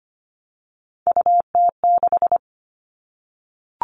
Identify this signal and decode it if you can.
Morse code, heard as a keyed tone, about 25 wpm, 700 Hz. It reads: UT6
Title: All Cut Numbers - 3-Digit 25wpm